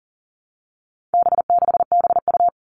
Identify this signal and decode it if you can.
Morse code, heard as a keyed tone, about 40 wpm, 700 Hz. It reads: B6BU
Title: All Cut Numbers - 4-Digit 40wpm